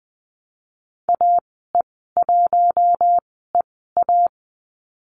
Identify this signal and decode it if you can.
Morse code, heard as a keyed tone, about 20 wpm, spaced as for 14 wpm, 700 Hz. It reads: AE1EA
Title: 082 Lesson 14g - Number 2 - Call Signs 20wpm